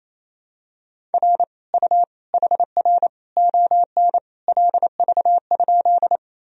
Morse code, heard as a keyed tone, about 28 wpm, 700 Hz. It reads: R U HR ON LV?
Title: Sentences from Top 400 Words and Abbreviations 28wpm